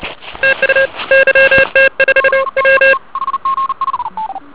Morse code, heard as a keyed tone, about 30 wpm, 1600 Hz, with noise. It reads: TU YT5 M